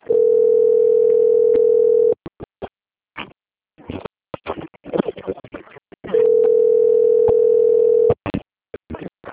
Ringback tone (North America), 440 + 480 Hz.
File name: out- - - - - .5515